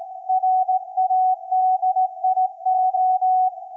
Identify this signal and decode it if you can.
Morse code, heard as a keyed tone, about 17 wpm, 750 Hz. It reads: RADIO
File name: iz7slz / VLF / SAQ 24-12-2024